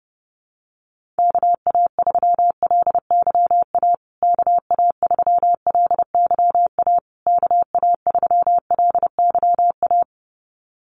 Morse code, heard as a keyed tone, about 30 wpm, 700 Hz. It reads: KA3LYA KA3LYA KA3LYA
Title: USA Call Signs Repeated 3x 30wpm